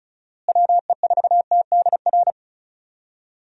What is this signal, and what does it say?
Morse code, heard as a keyed tone, about 36 wpm, 700 Hz. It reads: WE4TDR
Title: USA Call Signs - 6 Characters - Repeated 3x - 35wpm